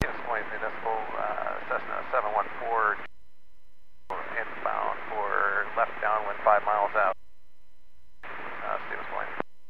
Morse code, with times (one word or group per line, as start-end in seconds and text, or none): none